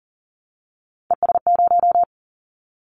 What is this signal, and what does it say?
Morse code, heard as a keyed tone, about 39 wpm, 700 Hz. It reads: ES0
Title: Callsign Prefixes for Top 100 DXCC - 40wpm